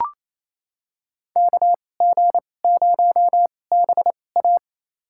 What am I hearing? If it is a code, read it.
Morse code, heard as a keyed tone, about 28 wpm, spaced as for 20 wpm, 700 Hz. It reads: KG0BA
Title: USA Call Signs - 2x Character Spacing - 28wpm